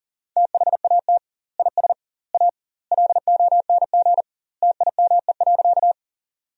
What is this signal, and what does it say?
Morse code, heard as a keyed tone, about 40 wpm, 700 Hz. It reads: THAT IS A LONG TIME.
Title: 2x Word-Spacing - Sentences from Top 100 Words - 40wpm